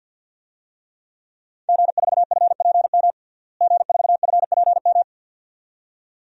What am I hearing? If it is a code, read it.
Morse code, heard as a keyed tone, about 52 wpm, 700 Hz. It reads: G4FPM G4FPM
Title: DXCC-100 Call Signs - 5 Characters - Repeated 2x - 50wpm